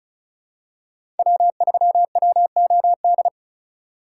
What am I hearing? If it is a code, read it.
Morse code, heard as a keyed tone, about 34 wpm, 700 Hz. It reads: W3WOD
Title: USA Call Signs - 6x Character Spacing - 35wpm